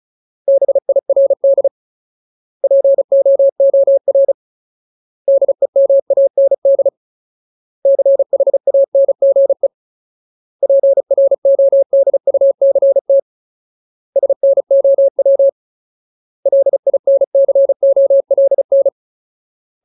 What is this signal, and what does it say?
Morse code, heard as a keyed tone, about 34 wpm, 550 Hz. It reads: BIRD POOR DEMAND CHANGE PRODUCT SNOW LINCOLN
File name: Ultimate Word List / 35WPM Ultimate